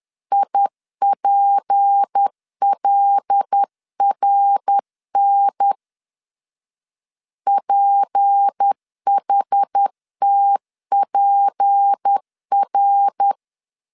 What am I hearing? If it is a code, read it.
Morse code, heard as a keyed tone, about 10 wpm, 800 Hz. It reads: IPLRN PHTPR
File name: LEZIONI / 6 PKX0 / 6 P 10wpm 2432